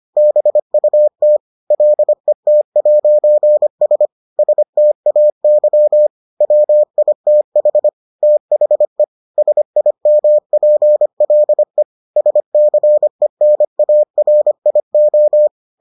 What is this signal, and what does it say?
Morse code, heard as a keyed tone, about 25 wpm, 600 Hz. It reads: BUT LET'S STAY WITH THE SIMPLE SCENARIO